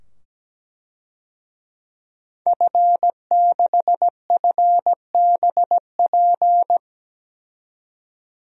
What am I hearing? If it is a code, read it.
Morse code, heard as a keyed tone, about 17 wpm, 700 Hz. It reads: F6FBP
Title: DXCC-100 Call Signs - 5 Characters - 17wpm